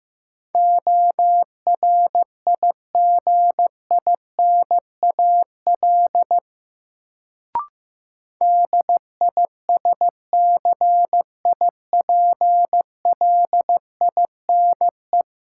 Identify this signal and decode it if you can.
Morse code, heard as a keyed tone, about 15 wpm, 700 Hz. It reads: ORIGINAL DISCIPLINE
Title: Sentences from Top 900 Words - New Word Drill - 15wpm